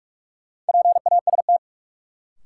Morse code, heard as a keyed tone, about 46 wpm, 700 Hz. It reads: PAST
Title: Sentences from Top 600 Words - New Word Drill - 45wpm